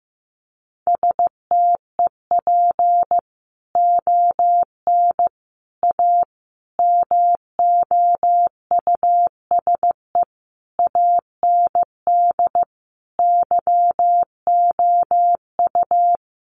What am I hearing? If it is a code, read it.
Morse code, heard as a keyed tone, about 15 wpm, 700 Hz. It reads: STEP ON A MOUSE AND YOU
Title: A Sound Of Thunder 15wpm